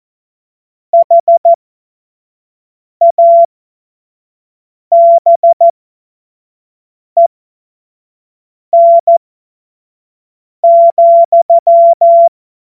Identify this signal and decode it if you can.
Morse code, heard as a keyed tone, about 14 wpm, spaced as for 5 wpm, 680 Hz. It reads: HABEN,
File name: Morsekurs / Witze / Witz 15. Februar 2022 25 F70